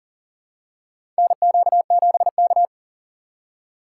Morse code, heard as a keyed tone, about 39 wpm, 700 Hz. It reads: NQ7K
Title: USA Call Signs - 4 Characters - 3x Character Spacing - 40wpm